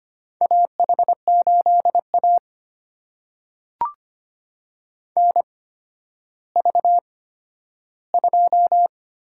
Morse code, heard as a keyed tone, about 25 wpm, spaced as for 12 wpm, 700 Hz. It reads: AH8A NV2
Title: USA Call Signs - 4 Characters - 8x Character Spacing - 25wpm